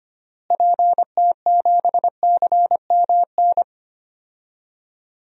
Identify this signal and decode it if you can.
Morse code, heard as a keyed tone, about 25 wpm, 700 Hz. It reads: PT7CMN
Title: DXCC-100 Call Signs - 6 Characters - Repeated 2x - 25wpm